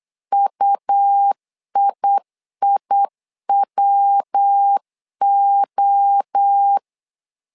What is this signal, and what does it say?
Morse code, heard as a keyed tone, about 8 wpm, 800 Hz. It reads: UIIWO
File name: LEZIONI / 4 Q 8wpm 2732